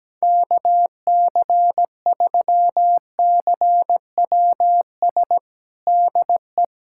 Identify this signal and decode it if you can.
Morse code, heard as a keyed tone, about 17 wpm, 700 Hz. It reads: KC3CWS DE
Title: USA Callsign DE Callsign - 6 Characters - 17wpm